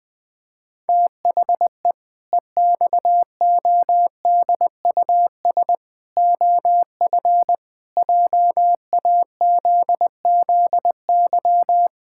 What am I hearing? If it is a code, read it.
Morse code, heard as a keyed tone, about 20 wpm, 700 Hz. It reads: THE EXODUS OF JAZZY
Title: Pangram Challenge v2 - 20wpm